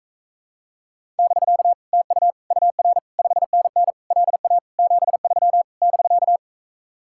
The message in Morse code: <BK> TU UR 5NN LA 73 <BK>

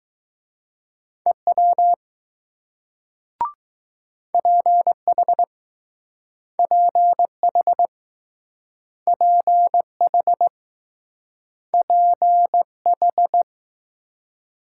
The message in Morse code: EW PH PH PH PH

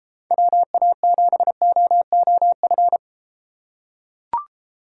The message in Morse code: WA7OOF

